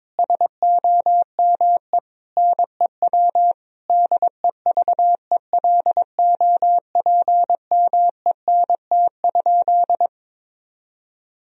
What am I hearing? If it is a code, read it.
Morse code, heard as a keyed tone, about 22 wpm, 700 Hz. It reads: SOME NEW DEVELOPMENT?
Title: Sentences from Top 1000 Words - Review Part 4 - 22wpm